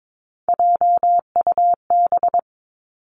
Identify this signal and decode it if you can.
Morse code, heard as a keyed tone, about 22 wpm, 700 Hz. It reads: JUB
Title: Three Challenging Letter-Number Combinations 22wpm